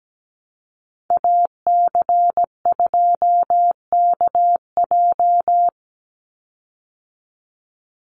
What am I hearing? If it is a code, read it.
Morse code, heard as a keyed tone, about 17 wpm, 700 Hz. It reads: AC2KJ
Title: Call Signs Repeated 2x - 17wpm